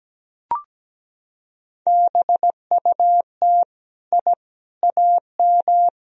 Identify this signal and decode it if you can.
Morse code, heard as a keyed tone, about 17 wpm, 700 Hz. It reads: BUT I AM
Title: Sentences from Top 1000 Words - Review Part 2 - 17wpm